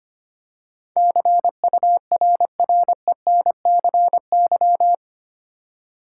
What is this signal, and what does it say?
Morse code, heard as a keyed tone, about 25 wpm, 700 Hz. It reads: CURRENCY
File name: Eight Letter Words - 25wpm